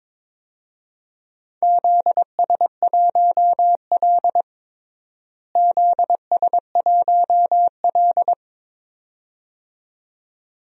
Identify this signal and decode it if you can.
Morse code, heard as a keyed tone, about 22 wpm, 700 Hz. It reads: ZS1L ZS1L
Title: DXCC-100 Call Signs - 4 Characters - Repeated 2x - 22wpm